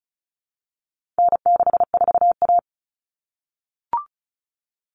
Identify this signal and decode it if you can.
Morse code, heard as a keyed tone, about 34 wpm, 700 Hz. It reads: N64A
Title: Common Cut Numbers - 4-Digit 35wpm